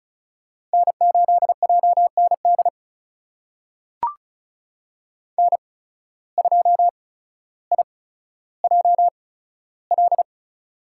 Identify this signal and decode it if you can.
Morse code, heard as a keyed tone, about 36 wpm, spaced as for 15 wpm, 700 Hz. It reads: N8JND N2IJL
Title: USA Call Signs - 5 Characters - 8x Character Spacing - 35wpm